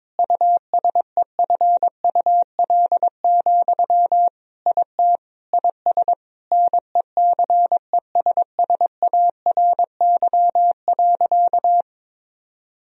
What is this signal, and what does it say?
Morse code, heard as a keyed tone, about 22 wpm, 700 Hz. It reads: USEFUL, IT IS NECESSARY.
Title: Sentences from Top 1000 Words - Review Part 4 - 22wpm